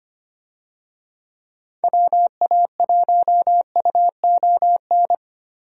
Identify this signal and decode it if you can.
Morse code, heard as a keyed tone, about 25 wpm, 700 Hz. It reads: WA1UON
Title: USA Call Signs - 6 Characters - Repeated 2x - 25wpm